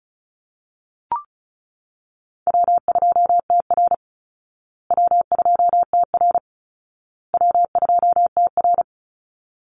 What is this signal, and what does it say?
Morse code, heard as a keyed tone, about 34 wpm, 700 Hz. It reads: W2TR W2TR W2TR
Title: Easy Call Signs Repeated 3x - 35wpm